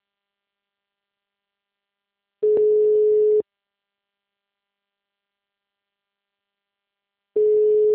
Ringback tone (Europe, 425 Hz).